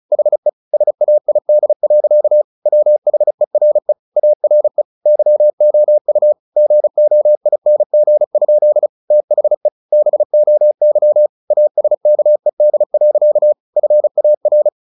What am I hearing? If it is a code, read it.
Morse code, heard as a keyed tone, about 34 wpm, 600 Hz. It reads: HE SAID. WHERE ARE YOU GOING? THE BOY ASKED. FAR
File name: the old man and the sea / the old man and the sea 1 35wpm